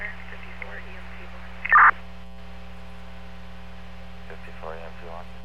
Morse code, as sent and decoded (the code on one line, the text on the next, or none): none